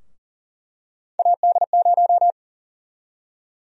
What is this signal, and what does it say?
Morse code, heard as a keyed tone, about 40 wpm, 700 Hz. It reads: AD0
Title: USA Call Sign Prefixes - Repeated 2x - 40wpm